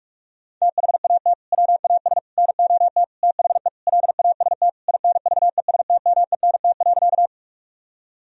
Morse code, heard as a keyed tone, about 44 wpm, 700 Hz. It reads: THAT WAS NOT THE LAST INVESTMENT.
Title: Sentences from Top 1000 Words - Review Part 4 - 45wpm